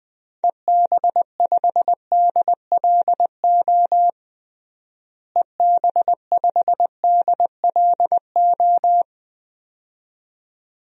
Morse code, heard as a keyed tone, about 20 wpm, 700 Hz. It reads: EB5DLO EB5DLO
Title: DXCC-100 Call Signs - 6 Characters - Repeated 2x - 20wpm